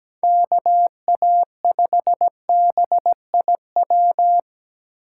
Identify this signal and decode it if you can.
Morse code, heard as a keyed tone, about 17 wpm, 700 Hz. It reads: KA5BIW
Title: USA Call Signs - 6 Characters - 3x Character Spacing - 17wpm